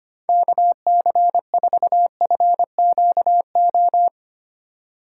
Morse code, heard as a keyed tone, about 25 wpm, 700 Hz. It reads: KC4FQO
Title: USA Call Signs - 3x Character Spacing - 25wpm